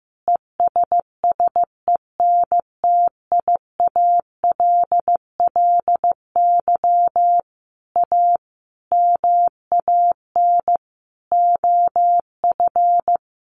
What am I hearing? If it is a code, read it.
Morse code, heard as a keyed tone, about 15 wpm, 700 Hz. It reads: ESSENTIALLY A MAN OF
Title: Sentences from Top 1300 Words 15wpm